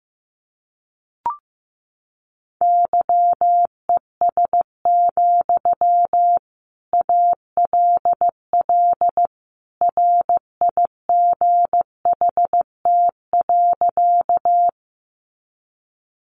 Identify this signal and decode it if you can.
Morse code, heard as a keyed tone, about 15 wpm, 700 Hz. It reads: YES, ALL RIGHT.
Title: Sentences from Top 500 Words 15wpm